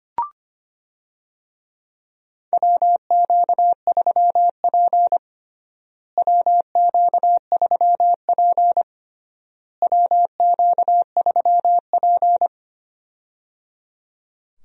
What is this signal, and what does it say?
Morse code, heard as a keyed tone, about 25 wpm, 700 Hz. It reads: WQ3P WQ3P WQ3P
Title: USA Call Signs - 4 Characters - Repeated 3x - 25wpm